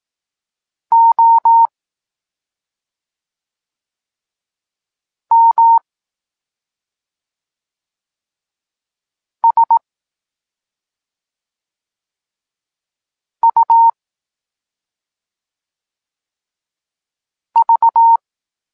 Morse code, heard as a keyed tone, about 18 wpm, spaced as for 2 wpm, 920 Hz. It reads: OMSUV
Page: Tekst 8 – doskonalenie odbioru poznanych liter